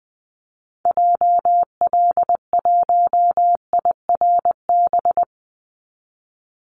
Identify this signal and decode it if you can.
Morse code, heard as a keyed tone, about 20 wpm, 700 Hz. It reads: JL1IRB
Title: Call Signs Repeated 3x - 20wpm